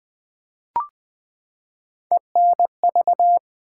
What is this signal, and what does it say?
Morse code, heard as a keyed tone, about 20 wpm, 700 Hz. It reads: ENV
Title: Build A Word - 3 Letters At A Time - 8-Letter Words - 20wpm